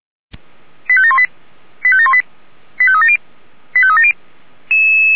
Five-tone selective call (ZVEI), heard as quick runs of five tones, tones of about 70 ms. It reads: 85718 85718 85370 85370